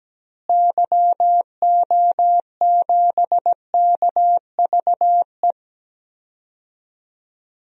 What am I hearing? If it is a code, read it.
Morse code, heard as a keyed tone, about 17 wpm, 700 Hz. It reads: YO7KVE
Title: DXCC-100 Call Signs - 6 Characters - Repeated 3x - 17wpm